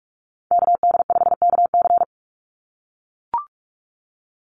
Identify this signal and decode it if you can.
Morse code, heard as a keyed tone, about 44 wpm, 700 Hz. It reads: KD5KC
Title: NA SOTA Activator Callsigns 45wpm